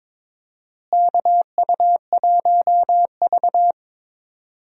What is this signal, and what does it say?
Morse code, heard as a keyed tone, about 22 wpm, 700 Hz. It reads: KU1V
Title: USA Call Signs - 4 Characters - Repeated 3x - 22wpm